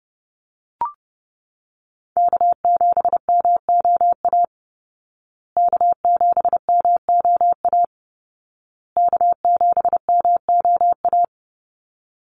Morse code, heard as a keyed tone, about 30 wpm, 700 Hz. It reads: K7MOA K7MOA K7MOA
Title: Call Signs Repeated 2x - 30wpm